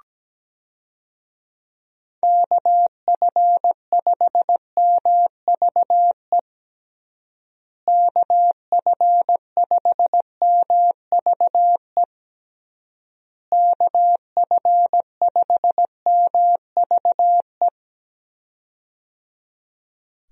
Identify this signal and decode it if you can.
Morse code, heard as a keyed tone, about 17 wpm, 700 Hz. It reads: KF5MVE KF5MVE KF5MVE
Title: USA Call Signs - 6 Characters - Repeated 3x - 17wpm